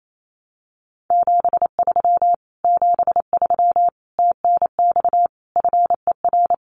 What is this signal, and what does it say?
Morse code, heard as a keyed tone, about 28 wpm, 700 Hz. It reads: 73 73 TNX FER
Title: Common QSO Phrases Encore 28wpm